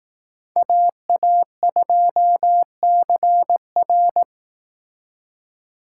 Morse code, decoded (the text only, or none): AA2CR